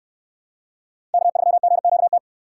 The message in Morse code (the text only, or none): K3GQT